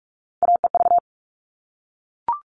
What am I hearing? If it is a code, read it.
Morse code, heard as a keyed tone, about 44 wpm, 700 Hz. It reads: AEV
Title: All Cut Numbers - 3-Digit 45wpm